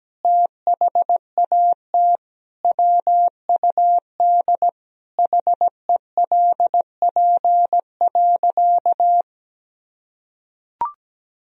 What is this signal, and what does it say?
Morse code, heard as a keyed tone, about 17 wpm, 700 Hz. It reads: THAT WUD HELP.
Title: Sentences from Top 200 Words and Abbreviations 17wpm